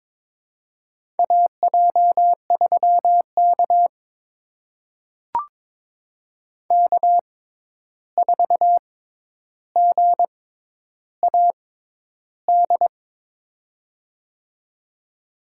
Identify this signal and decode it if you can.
Morse code, heard as a keyed tone, about 22 wpm, spaced as for 10 wpm, 700 Hz. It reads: AJ3K K4GAD
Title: USA Call Signs - 6x Character Spacing - 22wpm